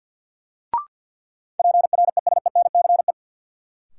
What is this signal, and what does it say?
Morse code, heard as a keyed tone, about 50 wpm, 700 Hz. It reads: PRESENCE
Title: Sets of 3 Words - New Word Drill - 50wpm